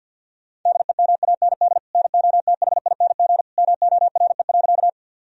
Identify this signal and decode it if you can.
Morse code, heard as a keyed tone, about 50 wpm, 700 Hz. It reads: DEMAND NOTHING MORE.
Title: Sentences from Top 1000 Words - Review Part 4 - 50wpm